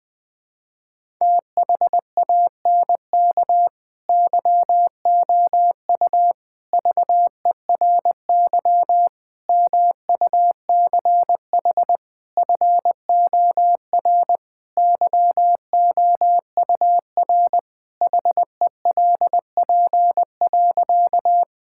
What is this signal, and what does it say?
Morse code, heard as a keyed tone, about 21 wpm, 700 Hz. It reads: THANK YOU VERY MUCH FOR YOUR HELP.
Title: Sentences with 4-Word Phrases - 20wpm